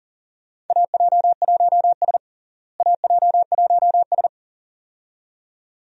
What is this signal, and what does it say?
Morse code, heard as a keyed tone, about 40 wpm, 700 Hz. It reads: AJ1S AJ1S
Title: USA Call Signs - 4 Characters - Repeated 2x - 40wpm